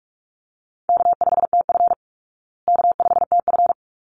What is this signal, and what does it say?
Morse code, heard as a keyed tone, about 44 wpm, 700 Hz. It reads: K5TF K5TF
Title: Easy Call Signs Repeated 2x - 45wpm